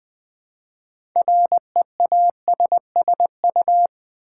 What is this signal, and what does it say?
Morse code, heard as a keyed tone, about 20 wpm, 690 Hz. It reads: REASSU
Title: Build A Word - 3 Letters At A Time - 8-Letter Words - 20wpm